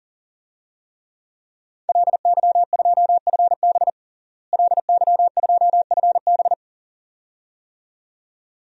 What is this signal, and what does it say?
Morse code, heard as a keyed tone, about 40 wpm, 700 Hz. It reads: LY2FB LY2FB
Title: DXCC-100 Call Signs - 5 Characters - Repeated 2x - 40wpm